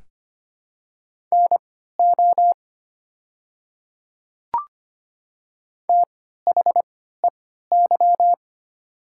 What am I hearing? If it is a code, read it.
Morse code, heard as a keyed tone, about 25 wpm, spaced as for 14 wpm, 700 Hz. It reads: NO THEY
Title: 3x Character Spacing - Top 200 Words - 25wpm